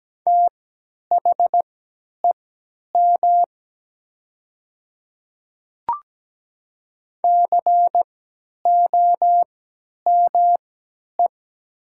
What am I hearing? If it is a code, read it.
Morse code, heard as a keyed tone, about 17 wpm, spaced as for 10 wpm, 700 Hz. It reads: THEM COME
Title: 3x Character Spacing - Top 200 Words - 17wpm